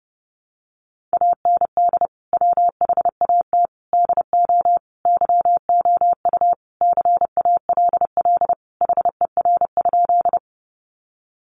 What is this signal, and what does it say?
Morse code, heard as a keyed tone, about 30 wpm, 700 Hz. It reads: AND WHAT DO YOU CALL HER?
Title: Sentences from Top 100 Words 30wpm